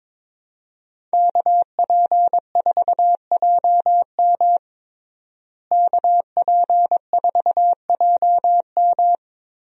Morse code, heard as a keyed tone, about 22 wpm, 700 Hz. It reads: KP4JM KP4JM
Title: DXCC-100 Call Signs - 5 Characters - Repeated 3x - 22wpm